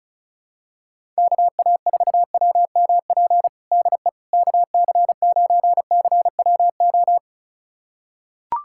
Morse code, heard as a keyed tone, about 34 wpm, 690 Hz. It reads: KA4WMP DE KC9CWO
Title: USA Callsign DE Callsign - 6 Characters - 35wpm